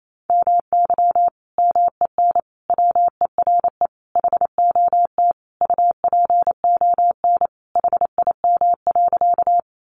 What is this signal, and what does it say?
Morse code, heard as a keyed tone, about 28 wpm, 700 Hz. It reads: MY MEN WERE HOT UPON HIM.
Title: Sentences from Top 1100 Words 28wpm